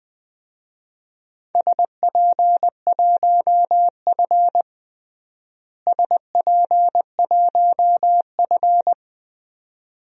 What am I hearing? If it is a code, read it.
Morse code, heard as a keyed tone, about 20 wpm, 700 Hz. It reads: SP1F SP1F
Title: DXCC-100 Call Signs - 4 Characters - Repeated 2x - 20wpm